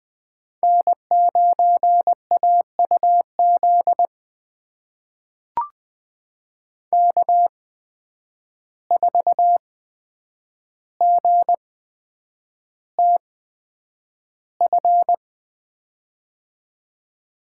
Morse code, heard as a keyed tone, about 20 wpm, spaced as for 9 wpm, 700 Hz. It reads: N9AUZ K4GTF
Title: USA Call Signs - 5 Characters - 8x Character Spacing - 20wpm